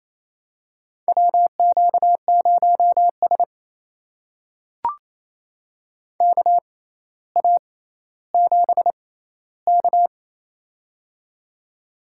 Morse code, heard as a keyed tone, about 28 wpm, spaced as for 14 wpm, 700 Hz. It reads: WQ0S KA7K
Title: USA Call Signs - 4 Characters - 6x Character Spacing - 28wpm